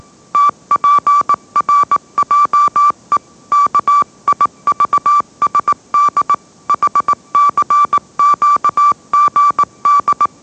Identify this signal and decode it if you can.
Morse code, heard as a keyed tone, about 18 wpm, 1160 Hz. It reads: TPRJE KIVSD HCQGD
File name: mrx / mrx082 20wpm